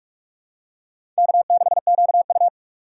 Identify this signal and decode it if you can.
Morse code, heard as a keyed tone, about 46 wpm, 700 Hz. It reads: K6QU
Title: USA Call Signs - 2x Character Spacing - 45wpm